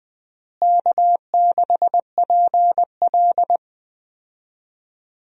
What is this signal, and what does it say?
Morse code, heard as a keyed tone, about 21 wpm, 700 Hz. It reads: K6PL
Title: USA Call Signs - 4 Characters - Repeated 2x - 20wpm